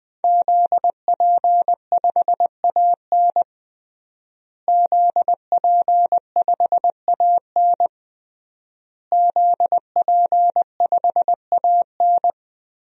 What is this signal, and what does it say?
Morse code, heard as a keyed tone, about 20 wpm, 700 Hz. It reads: ZP5AN ZP5AN ZP5AN
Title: DXCC-100 Call Signs - 5 Characters - Repeated 3x - 20wpm